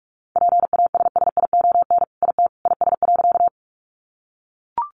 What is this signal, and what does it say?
Morse code, heard as a keyed tone, about 44 wpm, 700 Hz. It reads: PASSION IT IS.
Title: Sentences from Top 1300 Words 45wpm